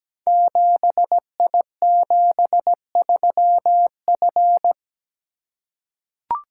Morse code, heard as a keyed tone, about 17 wpm, 700 Hz. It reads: 7I73F
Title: Code Groups - 5 Letter:Numbers - 17wpm